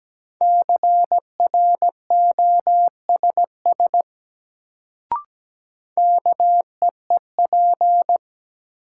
Morse code, heard as a keyed tone, about 17 wpm, 700 Hz. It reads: CROSS KEEP
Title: Sentences from Top 300 Words - New Word Drill - 17wpm